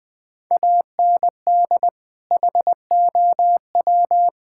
Morse code, heard as a keyed tone, about 21 wpm, 700 Hz. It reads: AND HOW
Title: Sentences from Top 1600 Words 20wpm